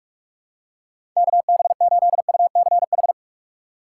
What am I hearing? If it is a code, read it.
Morse code, heard as a keyed tone, about 42 wpm, 690 Hz. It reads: KB8UCH
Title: USA Call Signs - 6 Characters - 3x Character Spacing - 45wpm